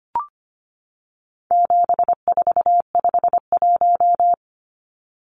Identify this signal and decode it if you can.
Morse code, heard as a keyed tone, about 25 wpm, 700 Hz. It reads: 7451
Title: Numbers - 4-Digit 25wpm